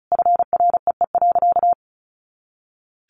Morse code, heard as a keyed tone, about 34 wpm, 700 Hz. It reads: FREE.
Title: Sentences from Top 500 Words 35wpm